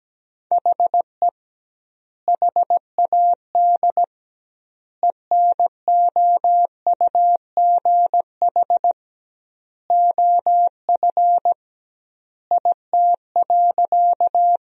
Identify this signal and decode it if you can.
Morse code, heard as a keyed tone, about 17 wpm, 700 Hz. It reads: HE HAD ENOUGH OF IT.
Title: 2x Word-Spacing - Sentences from Top 400 Words - 17wpm